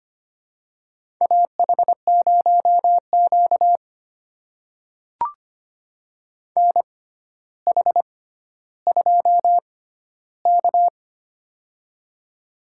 Morse code, heard as a keyed tone, about 25 wpm, spaced as for 13 wpm, 700 Hz. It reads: AH0Q NH2K
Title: USA Call Signs - 4 Characters - 6x Character Spacing - 25wpm